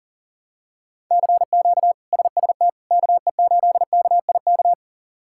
Morse code, heard as a keyed tone, about 40 wpm, 700 Hz. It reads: CQ SST KE8KIK
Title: K1USN SST QSO Practice - 40wpm